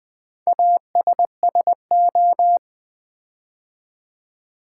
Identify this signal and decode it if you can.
Morse code, heard as a keyed tone, about 20 wpm, 700 Hz. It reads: ASSO
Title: Build A Word - 4 Letters At A Time - 11-Letter Words - 20wpm